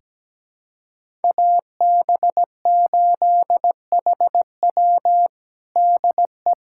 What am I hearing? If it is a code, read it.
Morse code, heard as a keyed tone, about 17 wpm, 700 Hz. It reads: AB8HW DE
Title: USA Callsign DE Callsign - 5 Characters - 17wpm